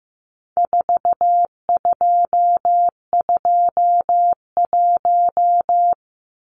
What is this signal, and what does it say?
Morse code, heard as a keyed tone, about 15 wpm, 700 Hz. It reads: 4221
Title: Numbers - 4-Digit 15wpm